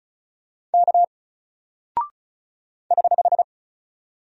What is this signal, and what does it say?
Morse code, heard as a keyed tone, about 36 wpm, 700 Hz. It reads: K <HH>
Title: Single Character (Letter:Number:Punctuation:Prosign) Alt 35wpm